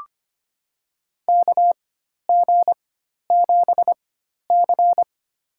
Morse code, heard as a keyed tone, about 25 wpm, spaced as for 12 wpm, 700 Hz. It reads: KG7C
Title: USA Call Signs - 4x Character Spacing - 25wpm